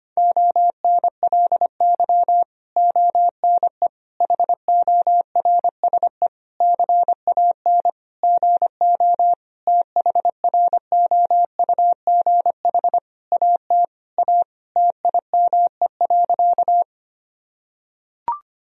Morse code, heard as a keyed tone, about 25 wpm, 700 Hz. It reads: ONLY ONE HORSE CAN GO THROUGH AT A TIME.
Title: Sentences from Top 1000 Words - Review Part 1 - 25wpm